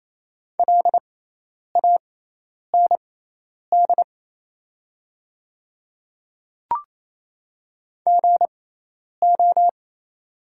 Morse code, heard as a keyed tone, about 27 wpm, spaced as for 10 wpm, 700 Hz. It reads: LAND GO
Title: 6x Character Spacing - Top 300 Words - 28wpm